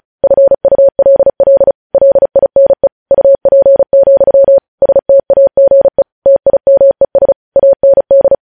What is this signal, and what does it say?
Morse code, heard as a keyed tone, about 36 wpm, 550 Hz. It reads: FULL LINE UP, STAGE TIMES AND
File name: podcast / News headlines-1x-35x35